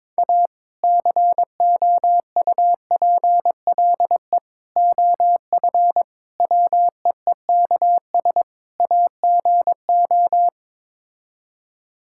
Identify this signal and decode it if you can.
Morse code, heard as a keyed tone, about 22 wpm, 700 Hz. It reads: A COUPLE OF WEEKS AGO
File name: practice / 22wpm / Build A Phrase - 5-Word Phrases - 22wpm